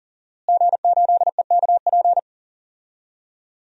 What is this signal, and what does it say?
Morse code, heard as a keyed tone, about 40 wpm, 700 Hz. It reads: G8EKP
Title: DXCC-100 Call Signs - 5 Characters - Repeated 3x - 40wpm